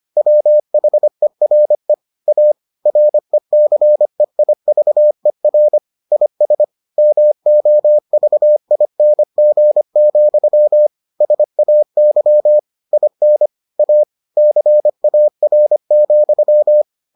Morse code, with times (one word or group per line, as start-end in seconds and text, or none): 0.2-1.9 WHERE
2.3-2.5 A
2.9-5.8 RECEIVER
6.1-6.6 IS
7.0-10.9 MOVING,
11.2-12.6 SAY
12.9-13.5 IN
13.8-14.0 A
14.4-16.8 CAR,